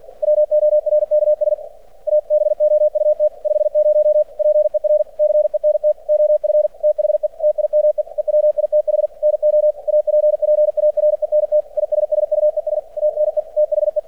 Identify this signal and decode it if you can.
Morse code, heard as a keyed tone, about 50 wpm, 600 Hz. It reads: WORKS ABOUT 50 PER CENT OF THE TIME EWITH NO APPARENT ISSUES TNE THE
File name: archived / CRSnet